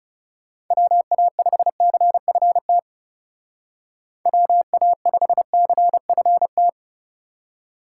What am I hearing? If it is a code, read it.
Morse code, heard as a keyed tone, about 35 wpm, 700 Hz. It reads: WA5CFT WA5CFT
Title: Speed Racing - USA 2x3 Call Signs 30wpm